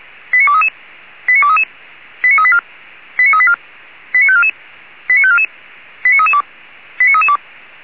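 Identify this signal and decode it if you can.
Five-tone selective call (ZVEI), heard as quick runs of five tones, tones of about 70 ms. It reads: 89233 89233 89384 89384 89544 89544 89332 89332
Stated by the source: ZVEI five-tone selective call 89288 received by SDR, from the Feuerwehr Orlamünde (fire brigade) audio server.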